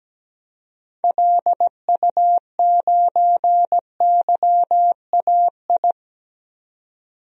Morse code, heard as a keyed tone, about 17 wpm, 700 Hz. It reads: LU9YAI
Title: DXCC-100 Call Signs - 6 Characters - 17wpm